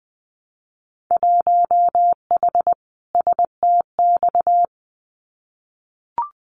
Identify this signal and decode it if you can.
Morse code, heard as a keyed tone, about 20 wpm, 700 Hz. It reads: 1H STX
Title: Winter Field Day Exchanges 20wpm